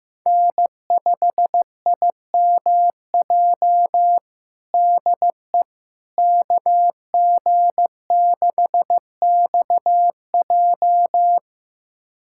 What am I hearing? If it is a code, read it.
Morse code, heard as a keyed tone, about 15 wpm, 700 Hz. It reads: N5IMJ DE KG6XJ
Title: USA Callsign DE Callsign - 5 Characters - 15wpm